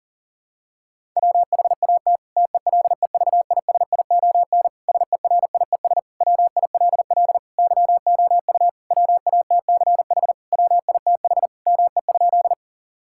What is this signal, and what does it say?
Morse code, heard as a keyed tone, about 40 wpm, 700 Hz. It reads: WHAT TELEVISION SERIES WILL YOU WATCH WITH ME?
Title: Sentences from Top 1000 Words - Review Part 5 - 40wpm